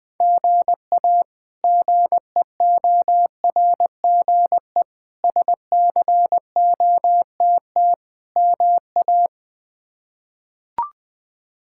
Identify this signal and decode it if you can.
Morse code, heard as a keyed tone, about 20 wpm, 700 Hz. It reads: GA GEORGE SCOTT MA
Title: K1USN SST QSO Practice - 20wpm